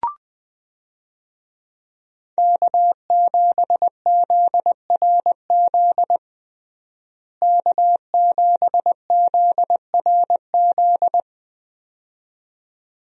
Morse code, heard as a keyed tone, about 20 wpm, 700 Hz. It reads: K7ZRZ K7ZRZ
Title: USA Call Signs - 5 Characters - Repeated 2x - 20wpm